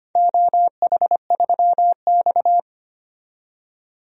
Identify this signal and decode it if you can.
Morse code, heard as a keyed tone, about 25 wpm, 700 Hz. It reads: OH3X
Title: DXCC-100 Call Signs - 4 Characters - 25wpm